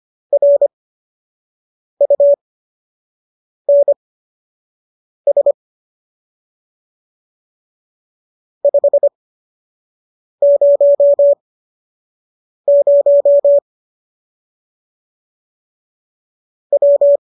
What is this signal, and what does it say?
Morse code, heard as a keyed tone, about 25 wpm, spaced as for 6 wpm, 570 Hz. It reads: RUNS 500 W